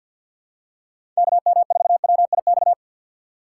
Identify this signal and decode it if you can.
Morse code, heard as a keyed tone, about 49 wpm, 700 Hz. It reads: KM4WIX
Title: USA Call Signs - 6 Characters - 3x Character Spacing - 50wpm